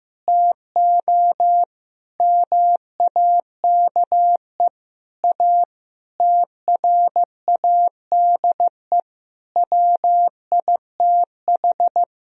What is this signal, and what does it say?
Morse code, heard as a keyed tone, about 15 wpm, 700 Hz. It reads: TO MAKE A TRADE WITH
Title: Sentences from Top 1000 Words - Review Part 4 - 15wpm